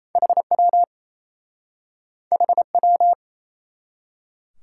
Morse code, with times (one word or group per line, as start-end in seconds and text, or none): 0.2-0.8 HW
2.3-3.1 HW